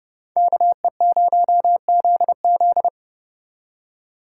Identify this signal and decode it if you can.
Morse code, heard as a keyed tone, about 38 wpm, 700 Hz. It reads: KE0ZZ